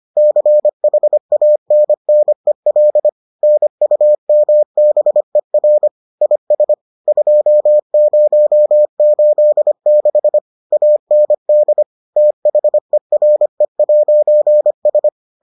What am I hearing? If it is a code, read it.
Morse code, heard as a keyed tone, about 25 wpm, 600 Hz. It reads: CHANNEL NUMBER IS 2086 AND THERE'S